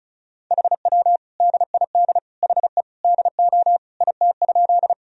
Morse code, heard as a keyed tone, about 36 wpm, 700 Hz. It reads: HW DID HE DO IT?
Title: Sentences from Top 100 Words and Abbreviations 35wpm